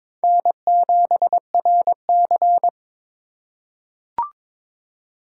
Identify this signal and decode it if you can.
Morse code, heard as a keyed tone, about 22 wpm, 700 Hz. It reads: N7RC
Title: USA Call Signs - 3x Character Spacing - 22wpm